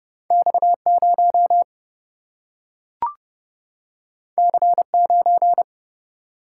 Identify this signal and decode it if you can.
Morse code, heard as a keyed tone, about 31 wpm, 700 Hz. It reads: X0 C9
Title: Single-Letter Single-Number 30wpm